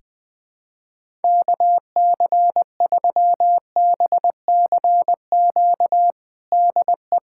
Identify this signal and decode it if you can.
Morse code, heard as a keyed tone, about 20 wpm, 700 Hz. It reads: KC3BCQ DE
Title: USA Callsign DE Callsign 20wpm